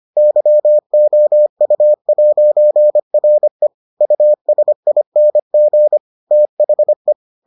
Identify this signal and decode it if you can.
Morse code, heard as a keyed tone, about 25 wpm, 600 Hz. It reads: YOU'RE USING THE